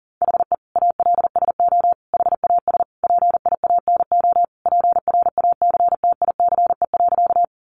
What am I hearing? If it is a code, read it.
Morse code, heard as a keyed tone, about 40 wpm, 700 Hz. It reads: HE ALSO HAS PIANO PRACTICE.
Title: Additional Easy Phrases 40wpm